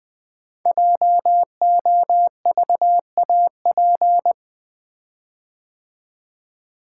Morse code, heard as a keyed tone, about 20 wpm, 700 Hz. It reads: JOVAP
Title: Code Groups - 5 Letters - 20wpm